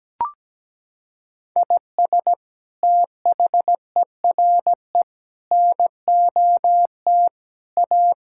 Morse code, heard as a keyed tone, about 17 wpm, 700 Hz. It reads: IS THERE NOT A
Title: Sentences from Top 1600 Words 17wpm